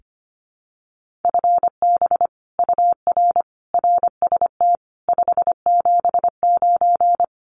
Morse code, heard as a keyed tone, about 25 wpm, 700 Hz. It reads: FB UR RST 579
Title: Common QSO Phrases Encore 25wpm